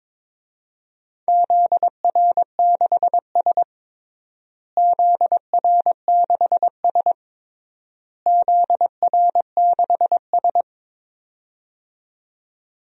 Morse code, heard as a keyed tone, about 22 wpm, 700 Hz. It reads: ZR6S ZR6S ZR6S
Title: DXCC-100 Call Signs - 4 Characters - Repeated 3x - 22wpm